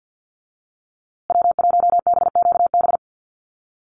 Morse code, heard as a keyed tone, about 49 wpm, 700 Hz. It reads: W1BQB
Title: USA Call Signs Repeated 2x 50wpm